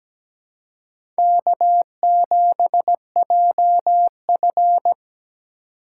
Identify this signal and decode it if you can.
Morse code, heard as a keyed tone, about 17 wpm, 700 Hz. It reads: K7JF
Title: USA Call Signs - 2x Character Spacing - 17wpm